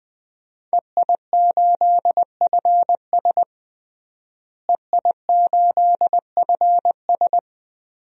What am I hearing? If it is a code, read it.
Morse code, heard as a keyed tone, about 21 wpm, 700 Hz. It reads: EI8FS EI8FS
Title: DXCC-100 Call Signs - 5 Characters - Repeated 3x - 20wpm